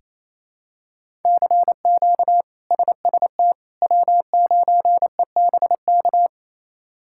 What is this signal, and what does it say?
Morse code, heard as a keyed tone, about 27 wpm, 700 Hz. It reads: CQ SST W9EBK
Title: K1USN SST QSO Practice - 28wpm